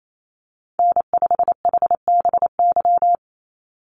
Morse code, heard as a keyed tone, about 28 wpm, 700 Hz. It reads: N5HBY
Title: USA Call Signs 28wpm